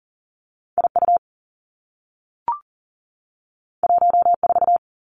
Morse code, heard as a keyed tone, about 41 wpm, 700 Hz. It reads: IU 14
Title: Two Challenging Letter-Number Combinations 40wpm